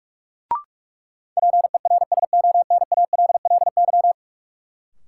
Morse code, heard as a keyed tone, about 46 wpm, 700 Hz. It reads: PERSONALLY
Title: Sentences from Top 1400 Words - New Word Drill - 45wpm